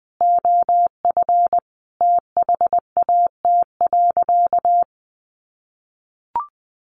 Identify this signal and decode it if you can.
Morse code, heard as a keyed tone, about 20 wpm, 700 Hz. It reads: OF THAT.
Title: Sentences from Top 300 Words 20wpm